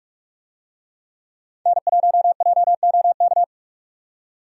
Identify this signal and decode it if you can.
Morse code, heard as a keyed tone, about 46 wpm, 700 Hz. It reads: N1JOK
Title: USA Call Signs - 5 Characters - 45wpm